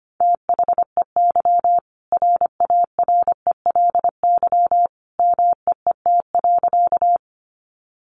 Morse code, heard as a keyed tone, about 25 wpm, 700 Hz. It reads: THEY RARELY MEET.
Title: Sentences from Top 1400 Words 25wpm